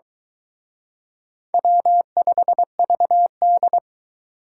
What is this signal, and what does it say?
Morse code, heard as a keyed tone, about 24 wpm, 700 Hz. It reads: W5VD